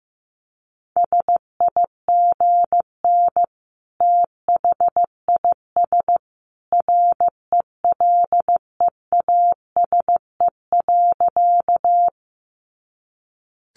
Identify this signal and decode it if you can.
Morse code, heard as a keyed tone, about 15 wpm, 700 Hz. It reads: SIGN THIS RELEASE.
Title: A Sound Of Thunder 15wpm